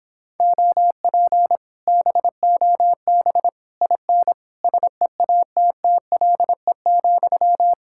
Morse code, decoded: OP BOB IN SEATTLE,

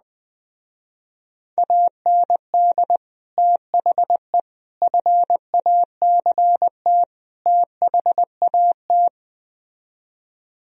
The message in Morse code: AND THE FACT THAT

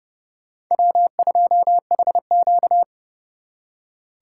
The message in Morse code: W2HQ

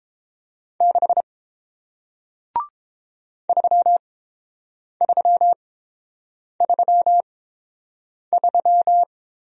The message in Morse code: 6 3333

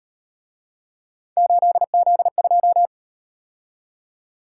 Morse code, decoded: 8Z2

